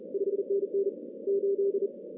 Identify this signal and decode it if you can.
Morse code, heard as a keyed tone, about 34 wpm, 410 Hz. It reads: 5NN 8
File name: 2015 cq ww cw / 2015 cqwww k5tr - 0-18